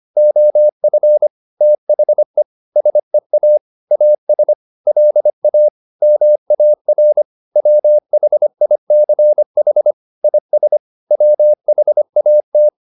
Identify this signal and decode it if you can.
Morse code, heard as a keyed tone, about 25 wpm, 600 Hz. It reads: OF THE SEA AS LA MAR WHICH IS WHAT
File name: the old man and the sea part 1 25wpm